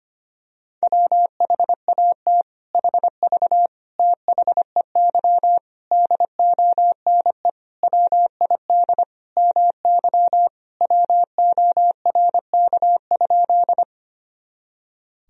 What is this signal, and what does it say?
Morse code, heard as a keyed tone, about 25 wpm, 700 Hz. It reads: WHAT HV THEY DONE WID MY WORK?
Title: Sentences from Top 500 Words and Abbreviations 25wpm